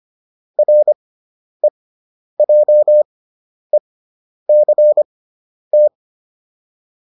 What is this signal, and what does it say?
Morse code, heard as a keyed tone, about 25 wpm, spaced as for 10 wpm, 600 Hz. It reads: REJECT
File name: RE6 10